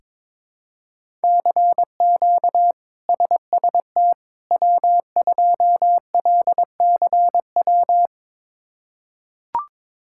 Morse code, heard as a keyed tone, about 22 wpm, 700 Hz. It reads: CQ SST W2LCW
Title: K1USN SST QSO Practice - 22wpm